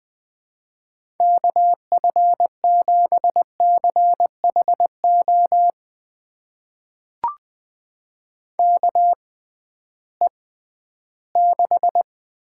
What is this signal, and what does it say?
Morse code, heard as a keyed tone, about 20 wpm, spaced as for 13 wpm, 700 Hz. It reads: KF7CHO KE6
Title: USA Call Signs - 6 Characters - 6x Character Spacing - 20wpm